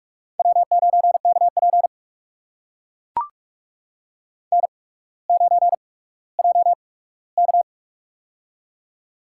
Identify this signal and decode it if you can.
Morse code, heard as a keyed tone, about 44 wpm, spaced as for 19 wpm, 700 Hz. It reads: W9KP N9JK
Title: USA Call Signs - 4 Characters - 8x Character Spacing - 45wpm